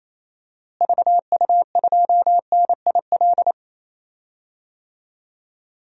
Morse code, heard as a keyed tone, about 28 wpm, 700 Hz. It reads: VU2NIL
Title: DXCC-100 Call Signs - 6 Characters - Repeated 2x - 28wpm